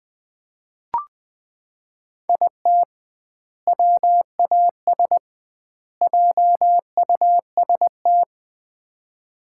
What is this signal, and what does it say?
Morse code, heard as a keyed tone, about 20 wpm, 700 Hz. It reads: IT WAS JUST
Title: 2x Word-Spacing - Sets of 3 Words - 20wpm